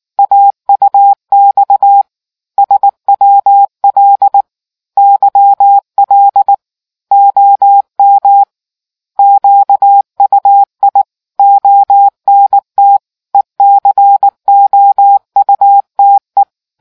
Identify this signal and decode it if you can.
Morse code, heard as a keyed tone, about 19 wpm, 800 Hz. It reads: AUX SWL YL OM QUI ONT ECOUTE